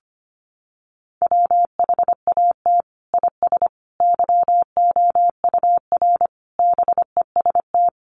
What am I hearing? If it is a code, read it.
Morse code, heard as a keyed tone, about 25 wpm, 700 Hz. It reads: WHAT IS YOUR BEST
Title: Sentences from Top 1200 Words 25wpm